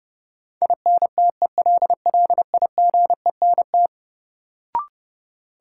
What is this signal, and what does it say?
Morse code, heard as a keyed tone, about 29 wpm, 700 Hz. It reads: INTELLIGENT
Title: Sentences from Top 1000 Words - Review Part 7 - New Word Drill - 30wpm